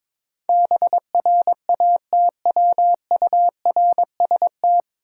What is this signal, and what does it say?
Morse code, heard as a keyed tone, about 22 wpm, 700 Hz. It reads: BRATWURST
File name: Difficult Words with Hints - 22wpm